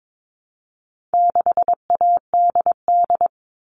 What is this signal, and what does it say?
Morse code, heard as a keyed tone, about 22 wpm, 700 Hz. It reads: 6ADD
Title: All Cut Numbers - 4-Digit 22wpm